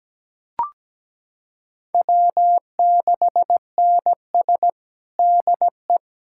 Morse code, heard as a keyed tone, about 17 wpm, 700 Hz. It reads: W6NS DE
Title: Practice QSO - Exchange 4 - Ending - 17wpm